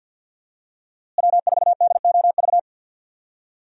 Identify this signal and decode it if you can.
Morse code, heard as a keyed tone, about 47 wpm, 700 Hz. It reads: W4DOV
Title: USA Call Signs - 5 Characters - Repeated 3x - 50wpm